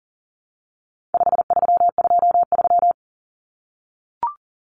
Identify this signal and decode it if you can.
Morse code, heard as a keyed tone, about 37 wpm, 700 Hz. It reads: 5323
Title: Numbers - 4-Digit 40wpm